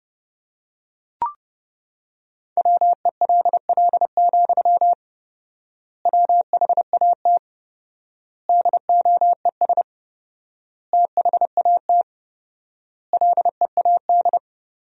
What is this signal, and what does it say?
Morse code, heard as a keyed tone, about 30 wpm, 700 Hz. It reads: WELL, WHAT DOES THAT LEAD
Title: 4x Word-Spacing - Sentences from Top 500 Words - 30wpm